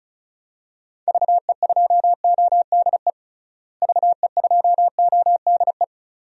Morse code, heard as a keyed tone, about 34 wpm, 700 Hz. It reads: VE2ODE VE2ODE
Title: DXCC-100 Call Signs - 6 Characters - Repeated 2x - 35wpm